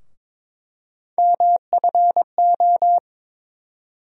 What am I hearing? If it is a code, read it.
Morse code, heard as a keyed tone, about 22 wpm, 700 Hz. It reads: MFO
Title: USA Call Sign Postfixes - 3 Letters 22wpm